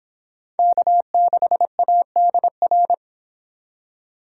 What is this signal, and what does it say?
Morse code, heard as a keyed tone, about 25 wpm, 700 Hz. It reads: K6ADR